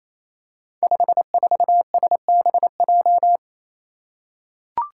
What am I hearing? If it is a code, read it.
Morse code, heard as a keyed tone, about 28 wpm, 700 Hz. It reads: 54SBJ
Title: Five Challenging Letter-Number Combinations 28wpm